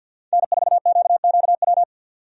Morse code, heard as a keyed tone, about 52 wpm, 700 Hz. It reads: N4QQW
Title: USA Call Signs - 5 Characters - 50wpm